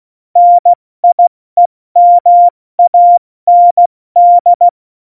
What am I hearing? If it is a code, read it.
Morse code, heard as a keyed tone, about 16 wpm, 700 Hz. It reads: NIEMAND
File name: Morsekurs / Witze / Witz 17. November 2020, 70 F80